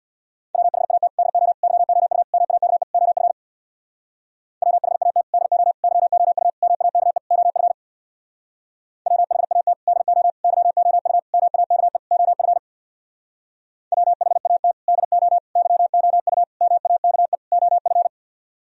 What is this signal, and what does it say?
Morse code, heard as a keyed tone, about 64 wpm, 700 Hz. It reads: WHAT DO YOU MAKE OF WHAT DO YOU MAKE OF WHAT DO YOU MAKE OF WHAT DO YOU MAKE OF